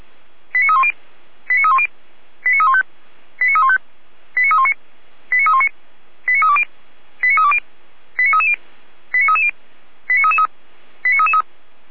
Five-tone selective call (ZVEI), heard as quick runs of five tones, tones of about 70 ms. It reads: 89310 89310 89315 89315 89319 89319 89320 89320 89330 89330 89333 89333